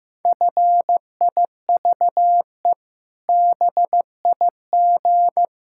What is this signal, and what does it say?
Morse code, heard as a keyed tone, about 16 wpm, 700 Hz. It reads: FIVE BIG
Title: Pangram Challenge v2 - 15wpm